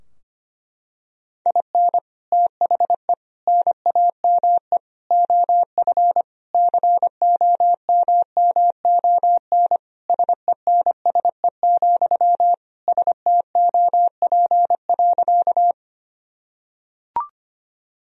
Morse code, heard as a keyed tone, about 25 wpm, 700 Hz. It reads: IN THE NAME OF COMMON SENSE, STOP.
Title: Sentences from Top 1000 Words - Review Part 4 - 25wpm